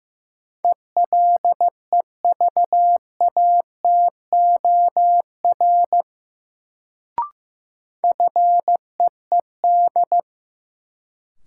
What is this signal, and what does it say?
Morse code, heard as a keyed tone, about 15 wpm, 700 Hz. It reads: ELEVATOR FEED
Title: Sentences from Top 1600 Words - New Word Drill - 15wpm